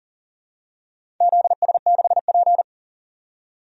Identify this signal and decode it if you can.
Morse code, heard as a keyed tone, about 40 wpm, 700 Hz. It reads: ZS6P
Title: DXCC-100 Call Signs - 4 Characters - 40wpm